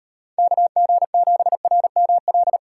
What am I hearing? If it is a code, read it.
Morse code, heard as a keyed tone, about 38 wpm, 700 Hz. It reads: KG7RML